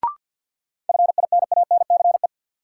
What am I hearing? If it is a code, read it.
Morse code, heard as a keyed tone, about 53 wpm, 700 Hz. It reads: FINANCE
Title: Sentences from Top 1500 Words - New Word Drill - 50wpm